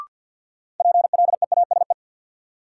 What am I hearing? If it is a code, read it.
Morse code, heard as a keyed tone, about 52 wpm, 700 Hz. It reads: PLEASE
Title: Sentences from Top 1500 Words - New Word Drill Alt - 50wpm